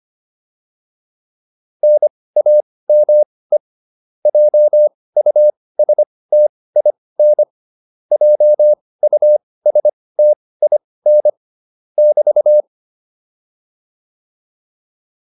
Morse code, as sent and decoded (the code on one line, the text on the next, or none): -. .- -- . / .--- ..- ... - .. -. / .--- ..- ... - .. -. / -...-
NAME JUSTIN JUSTIN <BT>